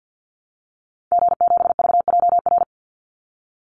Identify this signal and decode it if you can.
Morse code, heard as a keyed tone, about 49 wpm, 700 Hz. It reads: G7VJR
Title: Call Signs Repeated 3x - 50wpm